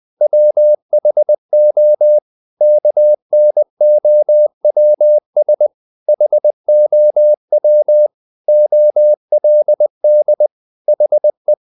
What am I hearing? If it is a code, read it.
Morse code, heard as a keyed tone, about 20 wpm, 600 Hz. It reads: WHO KNOWS HOW OLD HE
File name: the old man and the sea / the old man and the sea part 1 20wpm